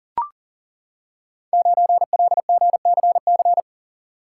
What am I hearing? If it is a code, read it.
Morse code, heard as a keyed tone, about 38 wpm, 700 Hz. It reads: 9LGCC
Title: Code Groups - 5 Letter:Numbers - 40wpm